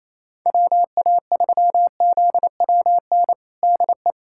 Morse code, heard as a keyed tone, about 28 wpm, 700 Hz. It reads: WA3ZWN DE
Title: USA Callsign DE Callsign 28wpm